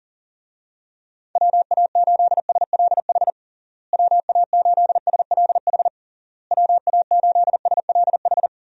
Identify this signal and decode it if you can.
Morse code, heard as a keyed tone, about 40 wpm, 700 Hz. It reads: WA8SLH WA8SLH WA8SLH
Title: USA Call Signs - 6 Characters - Repeated 3x - 40wpm